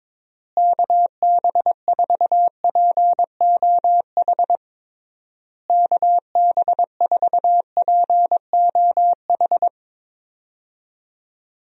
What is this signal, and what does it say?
Morse code, heard as a keyed tone, about 22 wpm, 700 Hz. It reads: KB4POH KB4POH
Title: USA Call Signs - 6 Characters - Repeated 3x - 22wpm